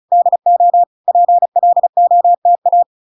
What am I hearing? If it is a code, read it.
Morse code, heard as a keyed tone, about 36 wpm, 700 Hz. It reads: DO PLOTA